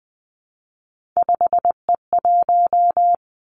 Morse code, heard as a keyed tone, about 20 wpm, 700 Hz. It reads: 5E1
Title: Three Challenging Letter-Number Combinations 20wpm